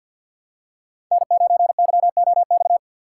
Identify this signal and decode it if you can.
Morse code, heard as a keyed tone, about 50 wpm, 700 Hz. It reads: N9YOX
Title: USA Call Signs - 6x Character Spacing - 50wpm